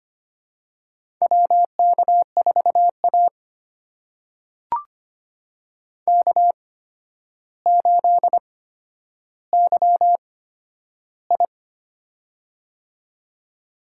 Morse code, heard as a keyed tone, about 25 wpm, spaced as for 11 wpm, 700 Hz. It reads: WK4A K8YI
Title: USA Call Signs - 4 Characters - 8x Character Spacing - 25wpm